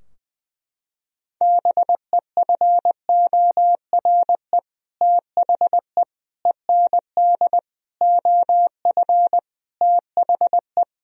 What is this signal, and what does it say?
Morse code, heard as a keyed tone, about 20 wpm, 700 Hz. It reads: BEFORE THE END OF THE
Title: Sentences from Top 1000 Words - Review Part 2 - 20wpm